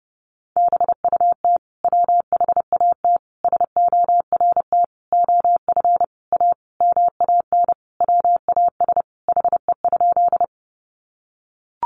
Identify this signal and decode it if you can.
Morse code, heard as a keyed tone, about 30 wpm, 700 Hz. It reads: BUT WHAT SORT OF A MAN WAS HE?
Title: Sentences from Top 1300 Words 30wpm